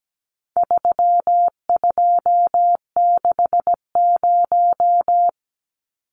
Morse code, heard as a keyed tone, about 17 wpm, 700 Hz. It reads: 3260
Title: Numbers - 4-Digit 17wpm